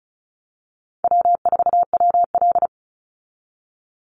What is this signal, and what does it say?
Morse code, heard as a keyed tone, about 34 wpm, 700 Hz. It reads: W4WL
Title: USA Call Signs 35wpm